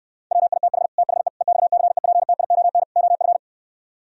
Morse code, heard as a keyed tone, about 68 wpm, 700 Hz. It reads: WITH THE EXCEPTION OF